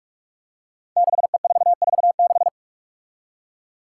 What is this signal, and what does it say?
Morse code, heard as a keyed tone, about 46 wpm, 700 Hz. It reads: 6E446
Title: Five Challenging Letter-Number Combinations 45wpm